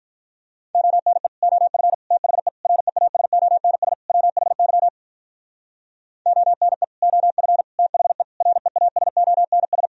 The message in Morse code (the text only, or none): ONE OF THE REASONS WHY ONE OF THE REASONS